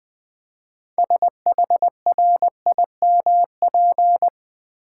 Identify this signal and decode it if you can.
Morse code, heard as a keyed tone, about 20 wpm, 700 Hz. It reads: SHRIMP
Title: Easy Words with Hints - 20wpm